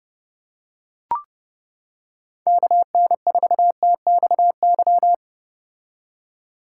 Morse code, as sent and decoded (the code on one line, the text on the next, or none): -.- -. ....- - -..- -.--
KN4TXY